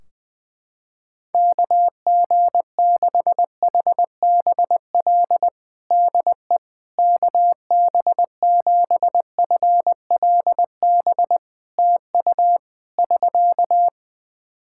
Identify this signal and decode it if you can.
Morse code, heard as a keyed tone, about 20 wpm, 700 Hz. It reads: KG6HBL DE KB7FLB TU <SK>
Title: Practice QSO - Exchange 4 - Ending - 20wpm